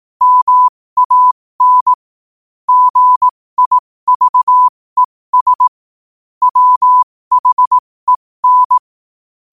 Morse code, heard as a keyed tone, about 18 wpm, spaced as for 15 wpm, 1000 Hz. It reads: MAN GIVES WHEN
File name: qotdfiles / 15wpm